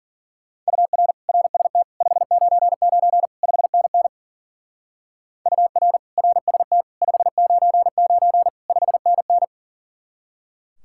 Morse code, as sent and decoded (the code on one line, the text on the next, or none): ..- .-. / .-. ... - / ..... ----. ----. / ..... -. -. / ..- .-. / .-. ... - / ..... ----. ----. / ..... -. -.
UR RST 599 5NN UR RST 599 5NN